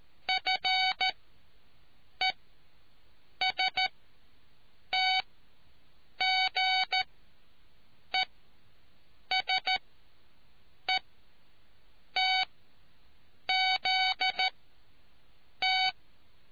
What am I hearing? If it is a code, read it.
Morse code, heard as a keyed tone, about 13 wpm, spaced as for 6 wpm, 750 Hz. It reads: FESTGESETZT